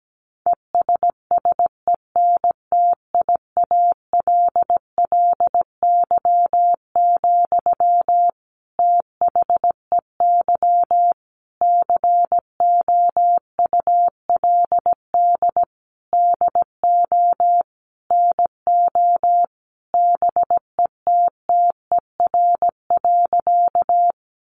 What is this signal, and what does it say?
Morse code, heard as a keyed tone, about 17 wpm, 700 Hz. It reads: ESSENTIALLY, THEY COULD DO NO BETTER.
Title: Sentences from Top 1300 Words 17wpm